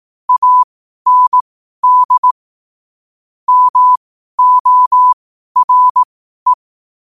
Morse code, heard as a keyed tone, about 18 wpm, spaced as for 12 wpm, 1000 Hz. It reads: AND MORE
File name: qotdfiles / 12wpm